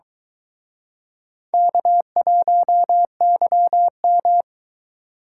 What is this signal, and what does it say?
Morse code, heard as a keyed tone, about 23 wpm, 700 Hz. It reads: K1YM